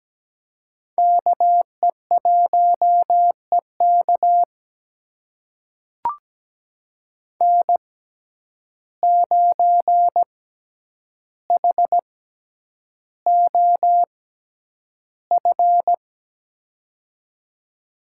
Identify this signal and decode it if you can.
Morse code, heard as a keyed tone, about 17 wpm, spaced as for 9 wpm, 700 Hz. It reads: KE1EK N9HOF
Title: USA Call Signs - 5 Characters - 6x Character Spacing - 17wpm